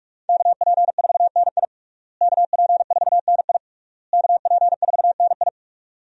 Morse code, heard as a keyed tone, about 46 wpm, 700 Hz. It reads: KP4NI KP4NI KP4NI
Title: USA Call Signs - 5 Characters - Repeated 3x - 45wpm